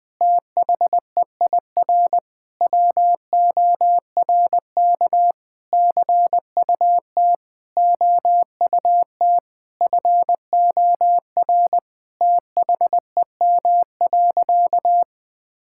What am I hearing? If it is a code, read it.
Morse code, heard as a keyed tone, about 21 wpm, 700 Hz. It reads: THEIR WORK CUT OUT FOR THEM.
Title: Sentences with 7-Word Phrases - 20wpm